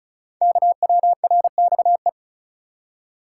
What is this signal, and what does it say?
Morse code, heard as a keyed tone, about 36 wpm, 690 Hz. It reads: KWRXE
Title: Code Groups - 5 Letters - 35wpm